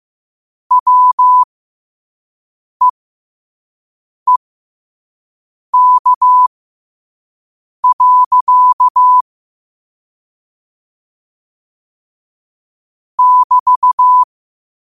Morse code, heard as a keyed tone, about 15 wpm, spaced as for 5 wpm, 1000 Hz. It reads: WEEK. <BT>
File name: qotdfiles / 5wpm